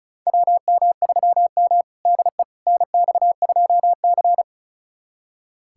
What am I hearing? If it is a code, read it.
Morse code, heard as a keyed tone, about 36 wpm, 700 Hz. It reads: WM3M DE NX2C
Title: USA Callsign DE Callsign - 4 Characters - 35wpm